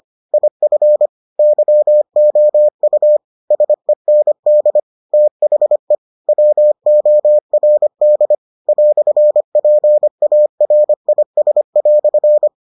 Morse code, heard as a keyed tone, about 25 wpm, 600 Hz. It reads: IF YOU SEND THE WORD "PARIS"